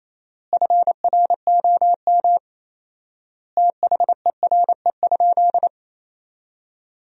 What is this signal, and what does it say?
Morse code, heard as a keyed tone, about 27 wpm, 700 Hz. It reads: FROM THERE?
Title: 4x Word-Spacing - Sentences from Top 100 Words - 28wpm